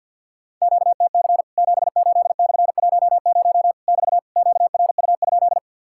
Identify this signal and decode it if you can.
Morse code, heard as a keyed tone, about 48 wpm, 700 Hz. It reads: QTC 78/10 <BT> QRU?
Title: Worked All Europe DX Contest - QTC Exchange - 50wpm